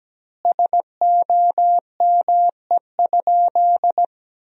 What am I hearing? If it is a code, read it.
Morse code, heard as a keyed tone, about 17 wpm, 700 Hz. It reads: SOME?
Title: 4x Word-Spacing - Sentences from Top 100 Words - 17wpm